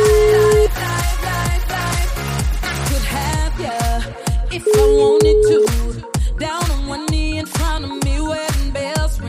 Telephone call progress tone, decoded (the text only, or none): ringback tone